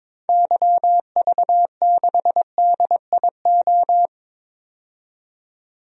Morse code, heard as a keyed tone, about 22 wpm, 700 Hz. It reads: YV6DIO
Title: DXCC-100 Call Signs - 6 Characters - 22wpm